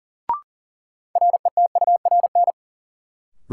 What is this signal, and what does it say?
Morse code, heard as a keyed tone, about 40 wpm, 700 Hz. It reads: RETURN
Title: Sentences from Top 800 Words - New Word Drill - 40wpm